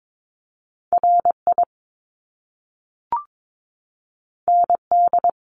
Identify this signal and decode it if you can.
Morse code, heard as a keyed tone, about 22 wpm, 700 Hz. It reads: RI ND
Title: Top Two Letter Combinations in Words 22wpm